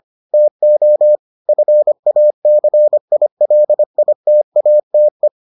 Morse code, heard as a keyed tone, about 25 wpm, 600 Hz. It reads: TO FACILITATE